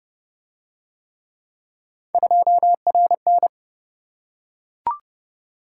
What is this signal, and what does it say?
Morse code, heard as a keyed tone, about 28 wpm, 700 Hz. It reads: 2RN